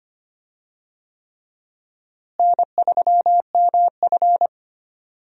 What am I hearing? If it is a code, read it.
Morse code, heard as a keyed tone, about 25 wpm, 700 Hz. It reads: N3MF
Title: USA Call Signs - 4 Characters - Repeated 3x - 25wpm